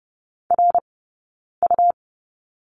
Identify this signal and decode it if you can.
Morse code, heard as a keyed tone, about 30 wpm, spaced as for 9 wpm, 690 Hz. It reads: RU
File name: Lesson 19d - Letter w - Two Characters 30wpm